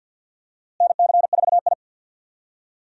Morse code, heard as a keyed tone, about 47 wpm, 700 Hz. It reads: NC4I
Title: USA Call Signs - 4 Characters - Repeated 2x - 50wpm